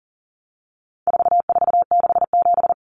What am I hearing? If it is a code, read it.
Morse code, heard as a keyed tone, about 40 wpm, 700 Hz. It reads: 4467
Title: Numbers - 4-Digit 40wpm